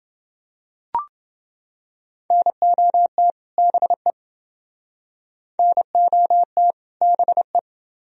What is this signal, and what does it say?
Morse code, heard as a keyed tone, about 28 wpm, 700 Hz. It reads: NOT BE NOT BE